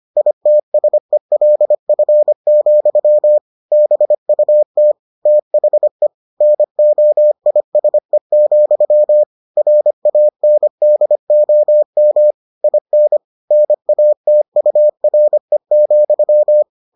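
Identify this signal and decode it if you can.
Morse code, heard as a keyed tone, about 25 wpm, 600 Hz. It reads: ITSELF, BUT THE NOISE, RANDOM IN NATURE,
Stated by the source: Morse edition of the Foundations of Amateur Radio episodes.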